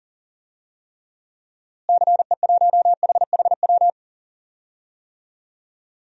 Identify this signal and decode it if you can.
Morse code, heard as a keyed tone, about 37 wpm, 700 Hz. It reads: CE1HHW
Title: DXCC-100 Call Signs - 6 Characters - Repeated 3x - 40wpm